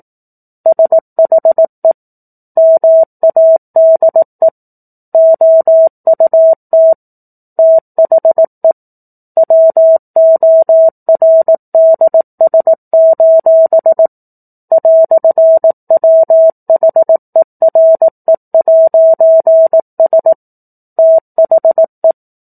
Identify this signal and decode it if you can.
Morse code, heard as a keyed tone, about 18 wpm, 650 Hz. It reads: SHE MADE OUT THE WORDS: "WHERE'S THE